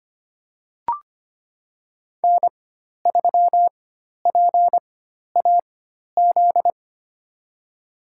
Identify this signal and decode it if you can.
Morse code, heard as a keyed tone, about 25 wpm, spaced as for 12 wpm, 700 Hz. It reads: N3PAZ
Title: USA Call Signs - 4x Character Spacing - 25wpm